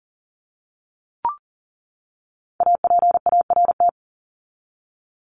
Morse code, heard as a keyed tone, about 40 wpm, 700 Hz. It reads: APART
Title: Top 1501-1750 Words 40wpm